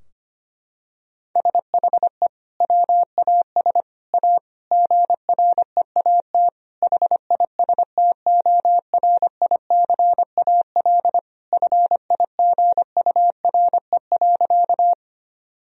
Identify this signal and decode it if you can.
Morse code, heard as a keyed tone, about 25 wpm, 700 Hz. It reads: SHE WAS A GREAT HISTORICAL FIGURE.
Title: Sentences from Top 1000 Words - Review Part 4 - 25wpm